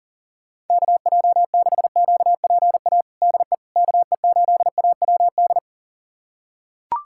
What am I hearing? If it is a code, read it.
Morse code, heard as a keyed tone, about 40 wpm, 700 Hz. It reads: KJ6QPA DE KE8AWD
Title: USA Callsign DE Callsign - 6 Characters - 40wpm